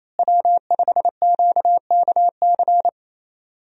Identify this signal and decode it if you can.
Morse code, heard as a keyed tone, about 28 wpm, 700 Hz. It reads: W5QKC
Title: USA Call Signs - 5 Characters - Repeated 3x - 28wpm